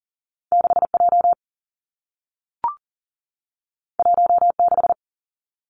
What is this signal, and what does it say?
Morse code, heard as a keyed tone, about 40 wpm, 700 Hz. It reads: 6 J 1 6
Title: Two Challenging Letter-Number Combinations 40wpm